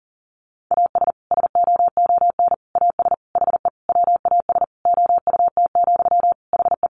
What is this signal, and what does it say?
Morse code, heard as a keyed tone, about 37 wpm, 700 Hz. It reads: AS SOON AS HE WAS OUT, HE
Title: Sentences from Top 1500 Words 40wpm